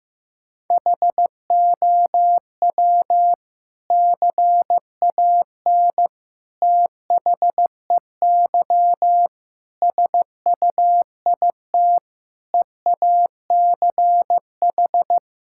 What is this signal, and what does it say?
Morse code, heard as a keyed tone, about 15 wpm, 700 Hz. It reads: HOW CAN THEY SUIT EACH
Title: Sentences from Top 1600 Words 15wpm